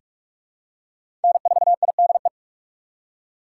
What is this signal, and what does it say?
Morse code, heard as a keyed tone, about 47 wpm, 700 Hz. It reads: N4IDE
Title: USA Call Signs - 5 Characters - 3x Character Spacing - 45wpm